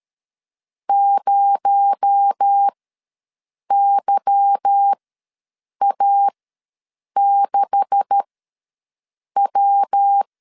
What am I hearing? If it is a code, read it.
Morse code, heard as a keyed tone, about 13 wpm, spaced as for 7 wpm, 800 Hz. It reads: 0YA6W
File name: LEZIONI / 9 286 ws 12wpm 3351